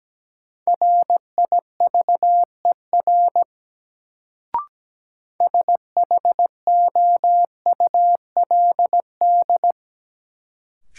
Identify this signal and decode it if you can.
Morse code, heard as a keyed tone, about 17 wpm, 700 Hz. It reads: RIVER SHOULD
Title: Sentences from Top 300 Words - New Word Drill - 17wpm